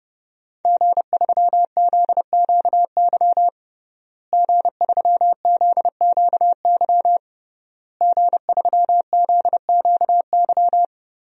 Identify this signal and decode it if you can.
Morse code, heard as a keyed tone, about 30 wpm, 700 Hz. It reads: G3ZQY G3ZQY G3ZQY
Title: DXCC-100 Call Signs - 5 Characters - Repeated 3x - 30wpm